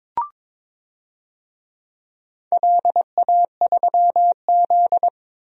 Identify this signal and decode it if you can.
Morse code, heard as a keyed tone, about 22 wpm, 700 Hz. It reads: LA3Z
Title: DXCC-100 Call Signs - 4 Characters - 22wpm